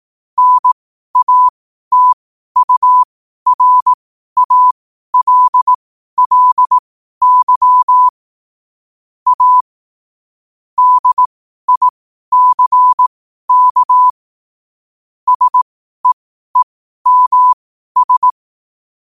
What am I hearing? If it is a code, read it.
Morse code, heard as a keyed tone, about 18 wpm, spaced as for 12 wpm, 1000 Hz. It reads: NATURALLY A DICK SEEMS
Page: Quotes for Sun, 27 Apr 2025 in Morse Code at 12 words per minute.